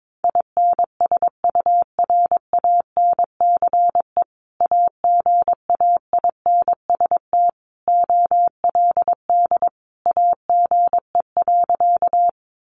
Morse code, heard as a keyed tone, about 22 wpm, 700 Hz. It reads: INSURANCE AGAINST OLD AGE.
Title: Sentences from Top 1100 Words 22wpm